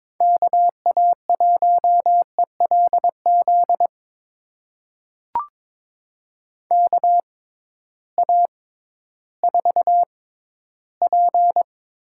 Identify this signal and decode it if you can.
Morse code, heard as a keyed tone, about 22 wpm, spaced as for 13 wpm, 700 Hz. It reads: KA1ELZ KA4P
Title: USA Call Signs - 6 Characters - 6x Character Spacing - 22wpm